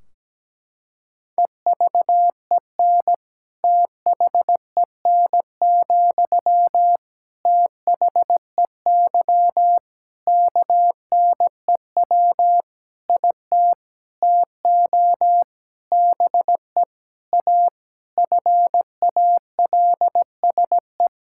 Morse code, heard as a keyed tone, about 17 wpm, 700 Hz. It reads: EVEN THEN, THEY KNEW IT TO BE A FALSE
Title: Sentences from Top 1600 Words 17wpm